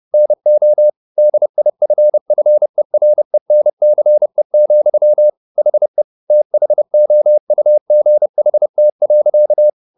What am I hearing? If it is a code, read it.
Morse code, heard as a keyed tone, about 30 wpm, 600 Hz. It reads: NO DIFFERENCE, HE THOUGHT.
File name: the old man and the sea / the old man and the sea 1 30wpm